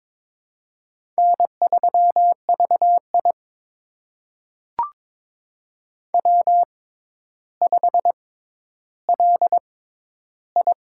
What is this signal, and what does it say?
Morse code, heard as a keyed tone, about 22 wpm, spaced as for 11 wpm, 700 Hz. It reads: N3VI W5LI
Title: USA Call Signs - 4 Characters - 6x Character Spacing - 22wpm